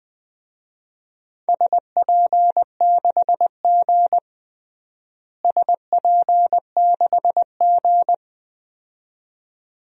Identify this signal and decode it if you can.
Morse code, heard as a keyed tone, about 20 wpm, 700 Hz. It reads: SP6G SP6G
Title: DXCC-100 Call Signs - 4 Characters - Repeated 2x - 20wpm